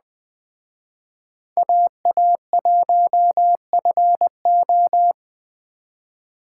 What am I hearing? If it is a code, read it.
Morse code, heard as a keyed tone, about 20 wpm, 700 Hz. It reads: AA1FO